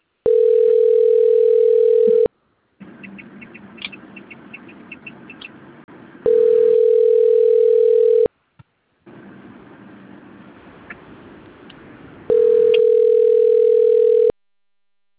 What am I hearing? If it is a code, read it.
Ringback tone (North America), 440 + 480 Hz.